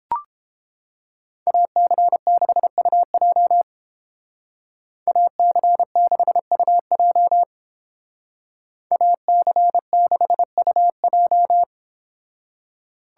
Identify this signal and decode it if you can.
Morse code, heard as a keyed tone, about 30 wpm, 700 Hz. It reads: AC6UJ AC6UJ AC6UJ